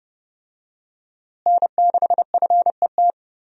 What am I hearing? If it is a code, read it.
Morse code, heard as a keyed tone, about 30 wpm, 700 Hz. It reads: N6FET
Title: USA Call Signs - 5 Characters - Repeated 2x - 30wpm